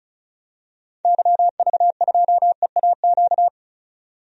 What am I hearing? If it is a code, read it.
Morse code, heard as a keyed tone, about 36 wpm, 700 Hz. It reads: YV2EAQ
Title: DXCC-100 Call Signs - 6 Characters - 35wpm